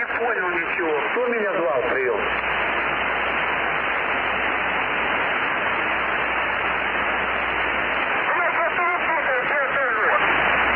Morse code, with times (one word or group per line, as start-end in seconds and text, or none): none